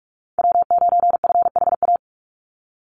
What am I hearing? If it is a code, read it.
Morse code, heard as a keyed tone, about 44 wpm, 700 Hz. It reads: W9FHA
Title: NA SOTA Chaser Callsigns 45wpm